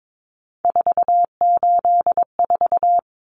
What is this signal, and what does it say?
Morse code, heard as a keyed tone, about 22 wpm, 700 Hz. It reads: 484
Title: Numbers - 3-Digit 22wpm